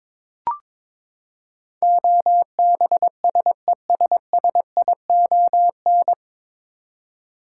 Morse code, heard as a keyed tone, about 22 wpm, 700 Hz. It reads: OBSESSION
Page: Morse Code Ninja practice 22wpm